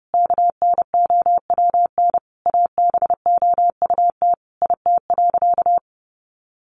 Morse code, heard as a keyed tone, about 30 wpm, 700 Hz. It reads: KNOWN ABOUT IT.
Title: Sentences from Top 1200 Words 30wpm